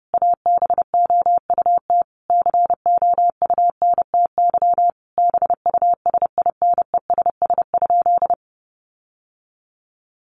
Morse code, heard as a keyed tone, about 30 wpm, 700 Hz. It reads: ABOUT COUNTY BUSINESS?
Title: Sentences from Top 1200 Words 30wpm